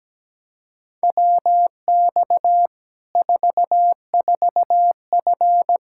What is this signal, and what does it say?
Morse code, heard as a keyed tone, about 17 wpm, 700 Hz. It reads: WX 44F
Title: K1USN SST QSO Practice 17wpm